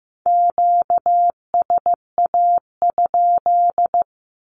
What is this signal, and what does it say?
Morse code, heard as a keyed tone, about 15 wpm, 700 Hz. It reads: QSA?
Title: Q-Codes with Spoken Meaning - 15wpm